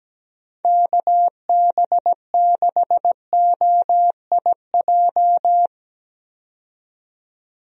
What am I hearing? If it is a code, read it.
Morse code, heard as a keyed tone, about 17 wpm, 700 Hz. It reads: KB6OIJ
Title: USA Call Signs - 6 Characters - 17wpm